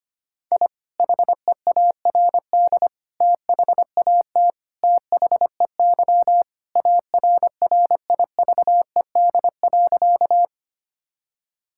Morse code, heard as a keyed tone, about 25 wpm, 700 Hz. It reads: I HEARD THAT THEY ARRIVED.
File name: practice / 25wpm / Sentences with 2-Word Phrases - 25wpm